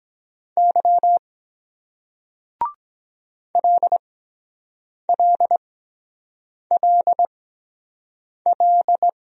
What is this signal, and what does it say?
Morse code, heard as a keyed tone, about 23 wpm, spaced as for 7 wpm, 700 Hz. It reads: Y LLLL